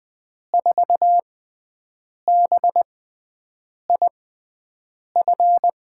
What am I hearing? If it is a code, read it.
Morse code, heard as a keyed tone, about 20 wpm, spaced as for 7 wpm, 700 Hz. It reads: 4BIF
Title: USA Call Signs - 6x Character Spacing - 20wpm